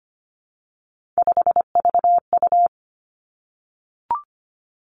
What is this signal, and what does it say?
Morse code, heard as a keyed tone, about 25 wpm, 700 Hz. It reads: 5VU
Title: Three Challenging Letter-Number Combinations 25wpm